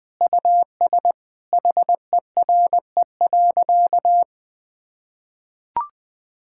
Morse code, heard as a keyed tone, about 20 wpm, 700 Hz. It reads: US HERE.
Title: Sentences with 5-Word Phrases - 20wpm